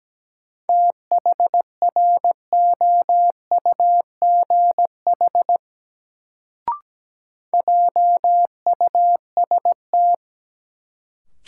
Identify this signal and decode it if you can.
Morse code, heard as a keyed tone, about 17 wpm, 700 Hz. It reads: THROUGH JUST
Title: Sentences from Top 200 Words - New Word Drill - 17wpm